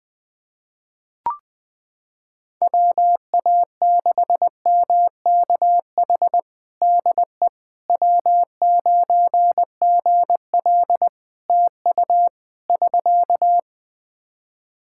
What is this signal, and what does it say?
Morse code, heard as a keyed tone, about 20 wpm, 700 Hz. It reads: WA6MKH DE W9GL TU <SK>
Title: Practice QSO - Exchange 4 - Ending - 20wpm